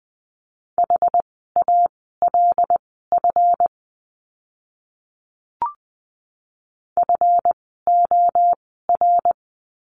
Morse code, heard as a keyed tone, about 20 wpm, spaced as for 14 wpm, 700 Hz. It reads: HALF FOR
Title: 139 Lesson 23f - Letter f - Words 20wpm